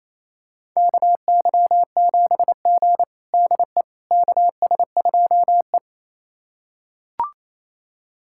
Morse code, heard as a keyed tone, about 28 wpm, 700 Hz. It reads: KY7G DE KS2E
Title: USA Callsign DE Callsign - 4 Characters - 28wpm